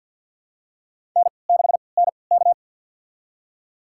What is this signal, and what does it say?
Morse code, heard as a keyed tone, about 49 wpm, spaced as for 28 wpm, 700 Hz. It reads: N6NK
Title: USA Call Signs - 4 Characters - 3x Character Spacing - 50wpm